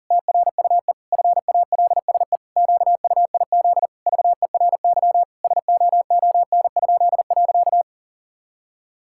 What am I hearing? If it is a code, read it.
Morse code, heard as a keyed tone, about 37 wpm, 700 Hz. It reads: TRUE FALSE QUIZ VERY SOON?.
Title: Pangram Challenge v2 - 40wpm